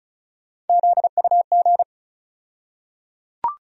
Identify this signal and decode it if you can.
Morse code, heard as a keyed tone, about 36 wpm, 700 Hz. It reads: ZUG
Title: USA Call Sign Postfixes - 3 Letters 35wpm